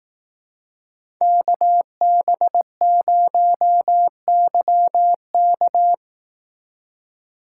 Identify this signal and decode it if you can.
Morse code, heard as a keyed tone, about 18 wpm, 700 Hz. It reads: KB0YK
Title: Speed Racing - USA 2x2 Call Signs 15wpm